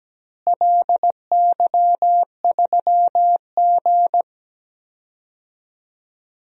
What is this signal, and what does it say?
Morse code, heard as a keyed tone, about 17 wpm, 700 Hz. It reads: LY3G
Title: DXCC-100 Call Signs - 4 Characters - Repeated 2x - 17wpm